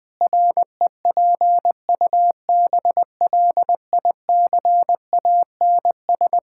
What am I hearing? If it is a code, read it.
Morse code, heard as a keyed tone, about 21 wpm, 700 Hz. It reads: REPUBLICANS
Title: Pangram Challenge v2 - 20wpm